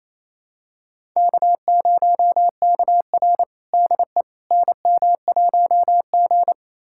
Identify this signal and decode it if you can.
Morse code, heard as a keyed tone, about 28 wpm, 700 Hz. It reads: K0KR DE NM1G
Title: USA Callsign DE Callsign - 4 Characters - 28wpm